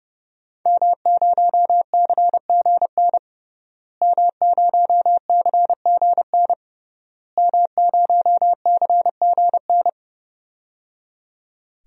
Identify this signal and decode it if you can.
Morse code, heard as a keyed tone, about 30 wpm, 700 Hz. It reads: M0CGN M0CGN M0CGN
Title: DXCC-100 Call Signs - 5 Characters - Repeated 3x - 30wpm